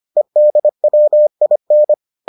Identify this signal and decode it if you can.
Morse code, heard as a keyed tone, about 25 wpm, 600 Hz. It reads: EDWIN